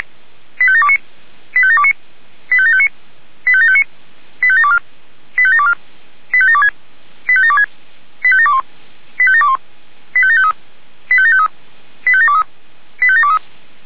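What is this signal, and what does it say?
Five-tone selective call (ZVEI), heard as quick runs of five tones, tones of about 70 ms. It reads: 85729 85729 85759 85759 85724 85724 85726 85726 85721 85721 85753 85753 85723 85723
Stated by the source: ZVEI five-tone selective call received on an SDR.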